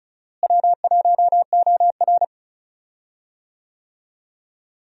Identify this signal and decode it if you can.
Morse code, heard as a keyed tone, about 34 wpm, 700 Hz. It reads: W1OR
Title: USA Call Signs - 4 Characters - Repeated 3x - 35wpm